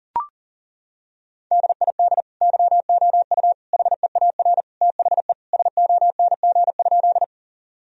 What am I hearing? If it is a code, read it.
Morse code, heard as a keyed tone, about 40 wpm, 700 Hz. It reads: DID YOU HEAR THE SONG?
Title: Sentences from Top 1000 Words - Review Part 2 - 40wpm